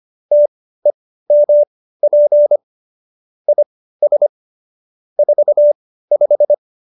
Morse code, heard as a keyed tone, about 25 wpm, spaced as for 15 wpm, 590 Hz. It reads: TEMP IS 45
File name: qso220b 15